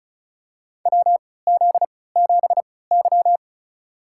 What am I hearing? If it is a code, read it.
Morse code, heard as a keyed tone, about 36 wpm, spaced as for 20 wpm, 700 Hz. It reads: WZ7Y
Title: USA Call Signs - 3x Character Spacing - 35wpm